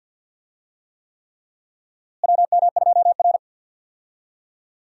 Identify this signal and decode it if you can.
Morse code, heard as a keyed tone, about 52 wpm, 700 Hz. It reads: WM2R
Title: USA Call Signs - 4 Characters - Repeated 3x - 50wpm